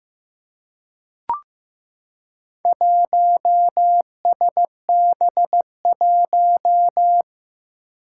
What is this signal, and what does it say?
Morse code, heard as a keyed tone, about 15 wpm, 700 Hz. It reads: 1SB1
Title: Four Challenging Letter-Number Combinations 15wpm